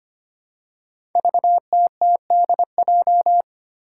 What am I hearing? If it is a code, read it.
Morse code, heard as a keyed tone, about 25 wpm, 700 Hz. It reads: VTTDJ
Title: Five Challenging Letter-Number Combinations 25wpm